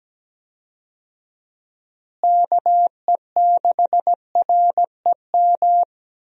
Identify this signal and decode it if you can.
Morse code, heard as a keyed tone, about 17 wpm, 700 Hz. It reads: KE6REM
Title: USA Call Signs - 6 Characters - Repeated 2x - 17wpm